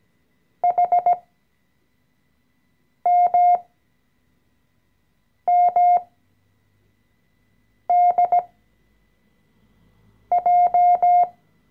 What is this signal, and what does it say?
Morse code, heard as a keyed tone, about 17 wpm, spaced as for 4 wpm, 710 Hz. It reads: HMMDJ
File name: CW-Online-Kurs / Anfänger / Lektion 014